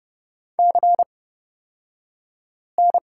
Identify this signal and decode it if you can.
Morse code, heard as a keyed tone, about 30 wpm, spaced as for 5 wpm, 700 Hz. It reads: CN
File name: Lesson 38d - Letter q - Two Characters 30wpm